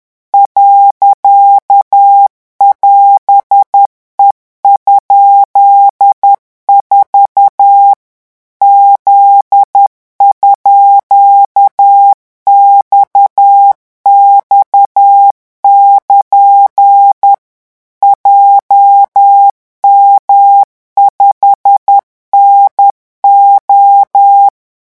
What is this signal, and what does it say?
Morse code, heard as a keyed tone, about 10 wpm, 800 Hz. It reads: .<AS>E?4 Z_XX<KN> JM5NO
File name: morse mp3 / revision16-vitesse 10 mots